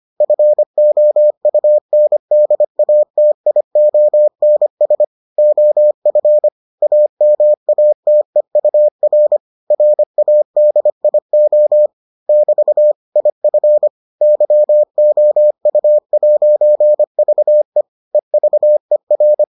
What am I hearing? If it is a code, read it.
Morse code, heard as a keyed tone, about 25 wpm, 600 Hz. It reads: FOUNDATIONS OF AMATEUR RADIO <BT> IF YOU'VE EVER